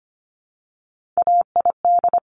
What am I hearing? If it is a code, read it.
Morse code, heard as a keyed tone, about 25 wpm, 700 Hz. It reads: AID
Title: Top Three Letter Combinations in Words 25wpm